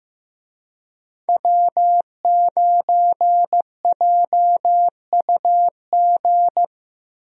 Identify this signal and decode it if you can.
Morse code, heard as a keyed tone, about 14 wpm, 700 Hz. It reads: W9JUG
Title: USA Call Signs - 5 Characters - Repeated 3x - 15wpm